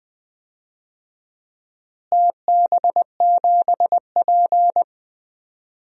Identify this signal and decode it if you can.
Morse code, heard as a keyed tone, about 20 wpm, 700 Hz. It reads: TB7P
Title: DXCC-100 Call Signs - 4 Characters - Repeated 3x - 20wpm